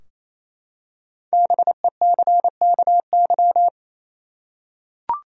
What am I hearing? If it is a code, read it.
Morse code, heard as a keyed tone, about 28 wpm, 700 Hz. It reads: BECKY
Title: USA Common Names - 28wpm